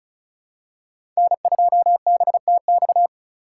Morse code, heard as a keyed tone, about 36 wpm, 700 Hz. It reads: N2BTX
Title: K1USN SST QSO Practice - 35wpm